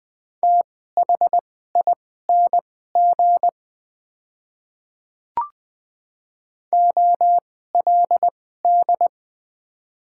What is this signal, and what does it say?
Morse code, heard as a keyed tone, about 20 wpm, spaced as for 14 wpm, 700 Hz. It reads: THING OLD
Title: 2x Character Spacing - Top 300 Words - 20wpm